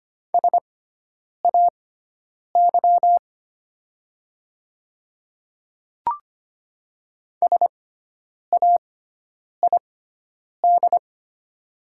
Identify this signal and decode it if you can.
Morse code, heard as a keyed tone, about 25 wpm, spaced as for 9 wpm, 700 Hz. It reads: SAY SAID
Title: 6x Character Spacing - Top 300 Words - 25wpm